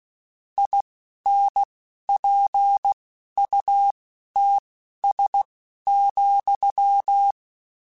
Morse code, heard as a keyed tone, about 16 wpm, spaced as for 11 wpm, 800 Hz. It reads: INPUTS,
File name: Technical Subjects / Technical Subjects-mp3s / Gates Intro @10